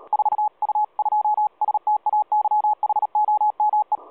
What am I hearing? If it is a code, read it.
Morse code, heard as a keyed tone, about 36 wpm, 840 Hz. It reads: 4U2STAYHOME
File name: 4U2STAYHOME 7016.2kHz